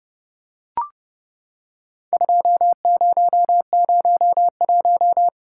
Morse code, heard as a keyed tone, about 32 wpm, 700 Hz. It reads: 2001
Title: Worked All Europe DX Contest - QTC Exchange - 30wpm